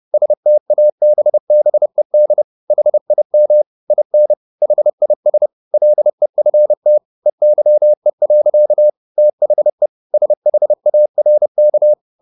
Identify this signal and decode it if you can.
Morse code, heard as a keyed tone, about 30 wpm, 600 Hz. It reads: STABBED HIM IN HIS LEFT EYE. THE SHARK